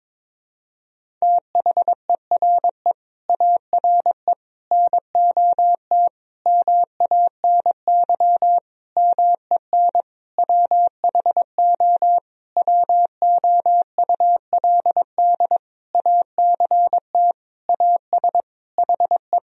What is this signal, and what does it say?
Morse code, heard as a keyed tone, about 22 wpm, 700 Hz. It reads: THERE ARE NOT MANY MEN WHO WOULD ACT AS HE
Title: Sentences from Top 1000 Words - Review Part 1 - 22wpm